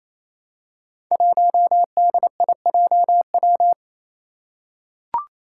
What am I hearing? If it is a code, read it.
Morse code, heard as a keyed tone, about 28 wpm, 700 Hz. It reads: 1DIJW
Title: Five Challenging Letter-Number Combinations 28wpm